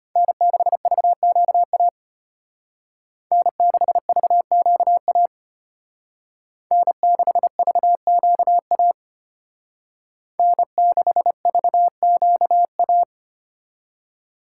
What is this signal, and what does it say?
Morse code, heard as a keyed tone, about 33 wpm, 700 Hz. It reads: N6VQA N6VQA N6VQA N6VQA